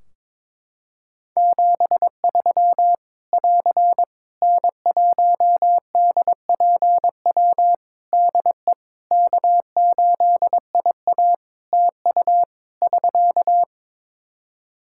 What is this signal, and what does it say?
Morse code, heard as a keyed tone, about 22 wpm, 700 Hz. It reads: 73 <AR> N1DPW DE K8IA TU <SK>
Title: K1USN SST QSO Practice 22wpm